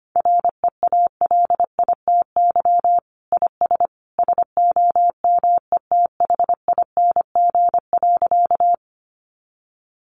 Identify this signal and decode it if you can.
Morse code, heard as a keyed tone, about 25 wpm, 700 Hz. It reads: REALITY IS SOMETHING.
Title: Sentences from Top 1200 Words 25wpm